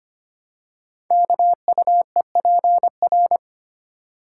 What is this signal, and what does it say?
Morse code, heard as a keyed tone, about 25 wpm, 700 Hz. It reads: KUEPR
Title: Code Groups - 5 Letters - 25wpm